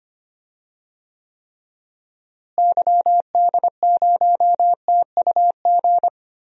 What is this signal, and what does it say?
Morse code, heard as a keyed tone, about 25 wpm, 700 Hz. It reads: YD0TUG
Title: DXCC-100 Call Signs - 6 Characters - Repeated 2x - 25wpm